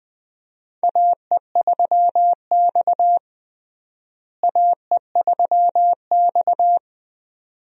Morse code, heard as a keyed tone, about 21 wpm, 700 Hz. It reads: AE3X AE3X
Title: USA Call Signs - 4 Characters - Repeated 2x - 20wpm